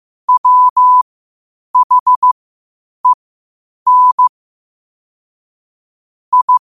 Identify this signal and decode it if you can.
Morse code, heard as a keyed tone, about 15 wpm, spaced as for 8 wpm, 1000 Hz. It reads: WHEN I
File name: qotdfiles / 8wpm